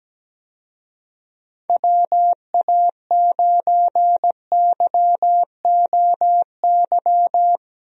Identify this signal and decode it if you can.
Morse code, heard as a keyed tone, about 17 wpm, 700 Hz. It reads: WA9YOY
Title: USA Call Signs - 6 Characters - Repeated 2x - 17wpm